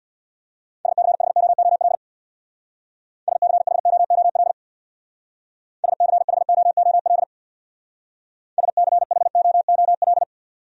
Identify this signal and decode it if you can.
Morse code, heard as a keyed tone, about 65 wpm, 700 Hz. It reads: SCHOOL SCHOOL SCHOOL SCHOOL